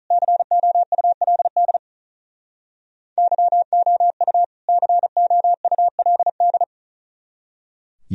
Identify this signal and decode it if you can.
Morse code, heard as a keyed tone, about 46 wpm, 700 Hz. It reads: COULD YOU COULD